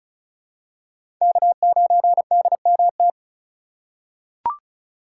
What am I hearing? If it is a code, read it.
Morse code, heard as a keyed tone, about 34 wpm, 700 Hz. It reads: K9DMT
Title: USA Call Signs - 5 Characters - 2x Character Spacing - 35wpm